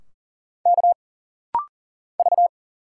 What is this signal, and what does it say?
Morse code, heard as a keyed tone, about 40 wpm, 700 Hz. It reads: K V
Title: Single Letters - Rapid-Fire 40wpm